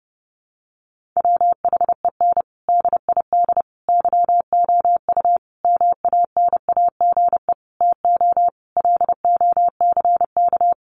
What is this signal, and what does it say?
Morse code, heard as a keyed tone, about 30 wpm, 700 Hz. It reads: WHEN DID YOU MANAGE TO LOCK
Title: Sentences from Top 1400 Words 30wpm